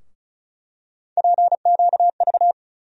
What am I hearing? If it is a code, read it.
Morse code, heard as a keyed tone, about 36 wpm, 700 Hz. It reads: PQV
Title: USA Call Sign Postfixes - 3 Letters 35wpm